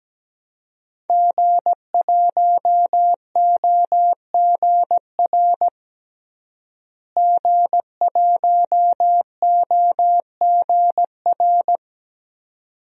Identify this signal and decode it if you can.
Morse code, heard as a keyed tone, about 17 wpm, 700 Hz. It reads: G1OGR G1OGR
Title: DXCC-100 Call Signs - 5 Characters - Repeated 2x - 17wpm